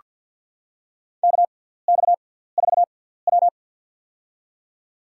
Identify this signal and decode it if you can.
Morse code, heard as a keyed tone, about 48 wpm, spaced as for 17 wpm, 700 Hz. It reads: KX4W
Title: USA Call Signs - 4 Characters - 6x Character Spacing - 50wpm